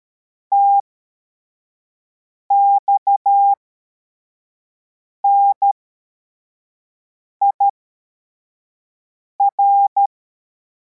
Morse code, heard as a keyed tone, about 13 wpm, spaced as for 4 wpm, 800 Hz. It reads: TXNIR
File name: formation / cw / Text34